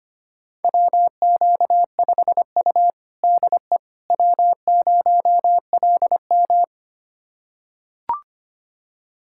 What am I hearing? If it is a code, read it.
Morse code, heard as a keyed tone, about 25 wpm, 700 Hz. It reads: WQ5U DE W0LM
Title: USA Callsign DE Callsign - 4 Characters - 25wpm